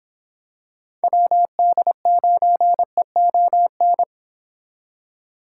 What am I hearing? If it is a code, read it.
Morse code, heard as a keyed tone, about 27 wpm, 700 Hz. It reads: WD9EON
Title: Speed Racing - USA Call Signs 22wpm